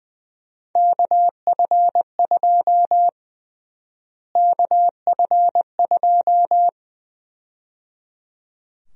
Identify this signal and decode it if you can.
Morse code, heard as a keyed tone, about 21 wpm, 700 Hz. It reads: KF2 KF2
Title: USA Call Sign Prefixes - Repeated 2x - 20wpm